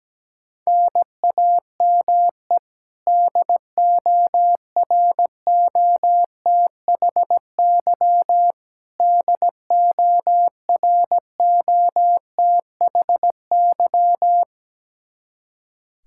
Morse code, heard as a keyed tone, about 17 wpm, 700 Hz. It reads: NAME DOROTHY DOROTHY
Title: Practice QSO - Exchange 1 - Beginning - 17wpm